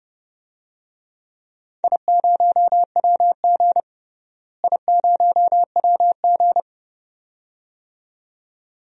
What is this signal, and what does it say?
Morse code, heard as a keyed tone, about 30 wpm, 700 Hz. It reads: I0WG I0WG
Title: DXCC-100 Call Signs - 4 Characters - Repeated 2x - 30wpm